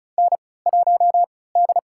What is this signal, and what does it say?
Morse code, heard as a keyed tone, about 34 wpm, spaced as for 20 wpm, 700 Hz. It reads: N1D
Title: USA Call Signs - 3x Character Spacing - 35wpm